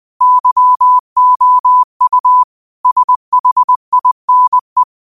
Morse code, heard as a keyed tone, about 20 wpm, 1000 Hz. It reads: YOU SHINE